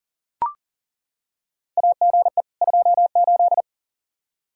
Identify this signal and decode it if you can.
Morse code, heard as a keyed tone, about 40 wpm, 700 Hz. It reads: AGE 28
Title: Practice QSO - Exchange 3 - More Info - 40wpm